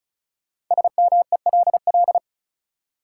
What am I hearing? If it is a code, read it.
Morse code, heard as a keyed tone, about 34 wpm, 700 Hz. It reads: SMELL
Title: Five Letter Words - 35wpm